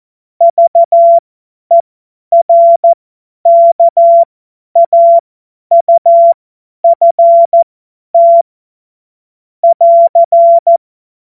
Morse code, heard as a keyed tone, about 14 wpm, spaced as for 10 wpm, 680 Hz. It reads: VERKAUFT <AR>
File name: Morsekurs / Witze / Witz 18. Mai 2021, 50 F70